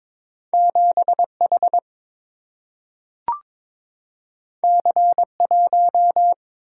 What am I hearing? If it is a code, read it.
Morse code, heard as a keyed tone, about 22 wpm, 700 Hz. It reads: 7 H C 1
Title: Single-Letter Single-Number 22wpm